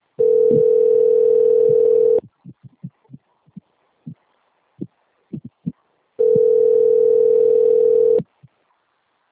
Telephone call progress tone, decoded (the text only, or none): ringback tone